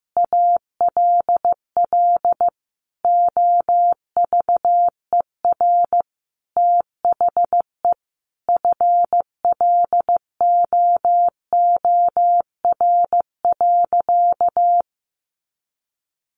Morse code, heard as a keyed tone, about 15 wpm, 700 Hz. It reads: ALL OVER THE FLOOR.
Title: Additional Easy Phrases 15wpm